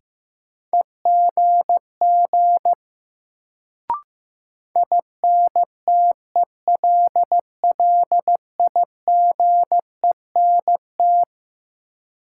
Morse code, heard as a keyed tone, about 15 wpm, 700 Hz. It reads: EGG INTELLIGENT
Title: Sentences from Top 1000 Words - Review Part 7 - New Word Drill - 15wpm